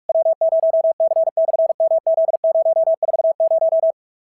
Morse code, heard as a keyed tone, about 46 wpm, 650 Hz. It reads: W0C/MZ040
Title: SOTA QSO Practice 45wpm